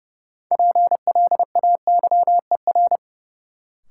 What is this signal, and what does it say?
Morse code, heard as a keyed tone, about 30 wpm, 700 Hz. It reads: PLAYER
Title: Sentences from Top 1000 Words - Review Part 4 - New Word Drill - 30wpm